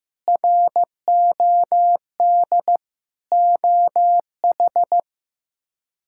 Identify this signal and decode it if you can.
Morse code, heard as a keyed tone, about 15 wpm, 700 Hz. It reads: ROD OH
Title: K1USN SST QSO Practice - 15wpm